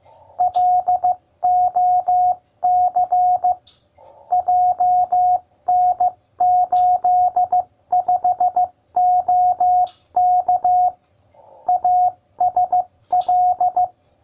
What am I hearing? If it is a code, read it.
Morse code, heard as a keyed tone, about 15 wpm, 720 Hz. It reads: LOC JN85OK ASL